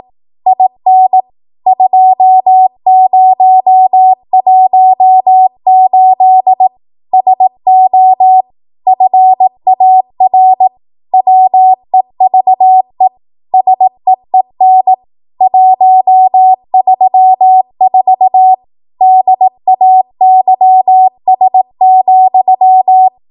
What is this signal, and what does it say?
Morse code, heard as a keyed tone, about 18 wpm, 750 Hz. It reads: IN 2018 SO FAR WEVE SEEN 134 DAYS,
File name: Archive / 18 WPM / 18WPM